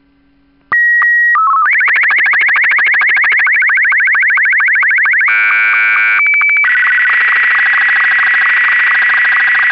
SSTV (slow-scan television), Martin 2.
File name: pagina videos audios sstv / test card colour martin 2